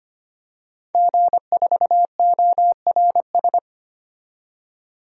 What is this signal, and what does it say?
Morse code, heard as a keyed tone, about 25 wpm, 700 Hz. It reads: G4ORS
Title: DXCC-100 Call Signs - 5 Characters - 25wpm